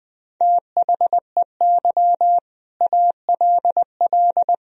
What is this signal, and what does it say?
Morse code, heard as a keyed tone, about 20 wpm, 700 Hz. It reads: THEY ALL
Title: Sentences from Top 1000 Words - Review Part 1 - 20wpm